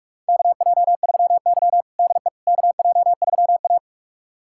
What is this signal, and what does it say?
Morse code, heard as a keyed tone, about 46 wpm, 700 Hz. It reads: KJ3Y DE KJ3A
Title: USA Callsign DE Callsign - 4 Characters - 45wpm